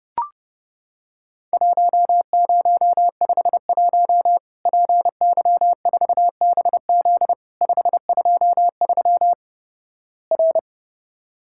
Morse code, heard as a keyed tone, about 30 wpm, 700 Hz. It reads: 1051 PY4BZ 523 R
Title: Worked All Europe DX Contest - QTC Exchange - 30wpm